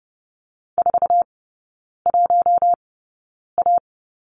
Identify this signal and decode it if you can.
Morse code, heard as a keyed tone, about 30 wpm, spaced as for 9 wpm, 700 Hz. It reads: 41A
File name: 081 Lesson 14e - Number 2 - Three Characters 30wpm